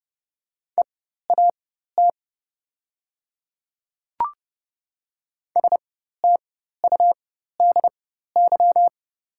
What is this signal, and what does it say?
Morse code, heard as a keyed tone, about 30 wpm, spaced as for 14 wpm, 700 Hz. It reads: EAT STUDY
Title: 4x Character Spacing - Top 300 Words - 30wpm